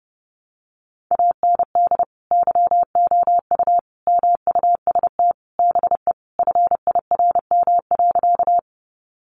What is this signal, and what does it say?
Morse code, heard as a keyed tone, about 30 wpm, 700 Hz. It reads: AND YOU MUST BE FIRM.
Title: Sentences from Top 1400 Words 30wpm